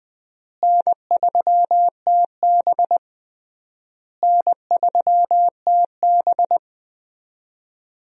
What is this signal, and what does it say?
Morse code, heard as a keyed tone, about 20 wpm, 700 Hz. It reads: N3TB N3TB
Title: USA Call Signs - 4 Characters - Repeated 3x - 20wpm